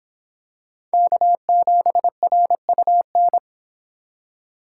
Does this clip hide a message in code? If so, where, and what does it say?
Yes: Morse code from 0.9 to 3.4 s, heard as a keyed tone, about 25 wpm, 700 Hz. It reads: K7RUN